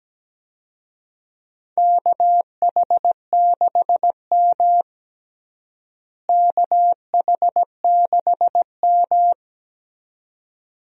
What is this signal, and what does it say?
Morse code, heard as a keyed tone, about 17 wpm, 700 Hz. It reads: KH6M KH6M
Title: USA Call Signs - 4 Characters - Repeated 2x - 17wpm